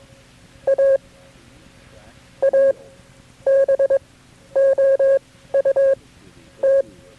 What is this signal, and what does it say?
Morse code, heard as a keyed tone, about 22 wpm, spaced as for 11 wpm, 540 Hz. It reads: A ABOUT
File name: KB6MT-tape3-side1